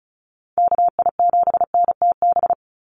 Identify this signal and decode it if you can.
Morse code, heard as a keyed tone, about 34 wpm, 700 Hz. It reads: KI7NTB
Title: USA Call Signs 35wpm